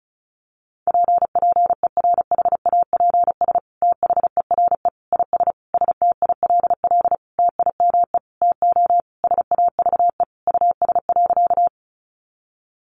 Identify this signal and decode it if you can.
Morse code, heard as a keyed tone, about 34 wpm, 700 Hz. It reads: PPERHAPS THERE IS STILL TIME TO SAVE US.
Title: Sentences from Top 1100 Words 35wpm